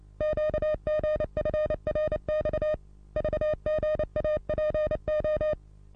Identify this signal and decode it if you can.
Morse code, heard as a keyed tone, about 28 wpm, 610 Hz. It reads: QGFRX VGAPO
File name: repos / Morsekurs / DJ4UF Morsen27